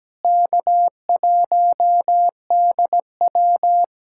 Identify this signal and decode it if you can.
Morse code, heard as a keyed tone, about 17 wpm, 700 Hz. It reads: K1DW
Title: USA Callsign DE Callsign - 4 Characters - 17wpm